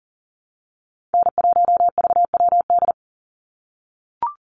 Morse code, heard as a keyed tone, about 40 wpm, 700 Hz. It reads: N1VWD
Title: NA SOTA Chaser Callsigns 40wpm